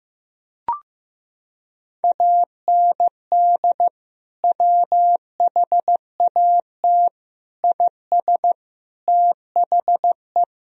Morse code, heard as a keyed tone, about 16 wpm, 700 Hz. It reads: AND WHAT IS THE
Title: Sentences from Top 1000 Words - Review Part 3 - 15wpm